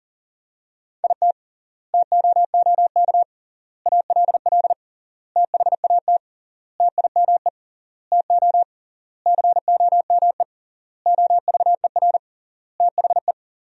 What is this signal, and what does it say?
Morse code, heard as a keyed tone, about 40 wpm, 700 Hz. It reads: IT TOOK ALL THAT TIME TO COME OVER THE
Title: 3x Word-Spacing - Sentences from Top 300 Words - 40wpm